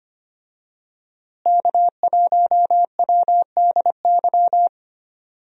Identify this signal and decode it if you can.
Morse code, heard as a keyed tone, about 25 wpm, 700 Hz. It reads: K1WDY
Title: USA Call Signs - 5 Characters - Repeated 2x - 25wpm